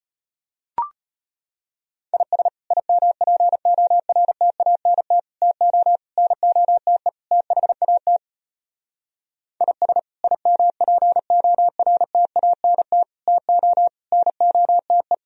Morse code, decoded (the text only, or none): IS IMPORTANT TO NOTE THAT IS IMPORTANT TO NOTE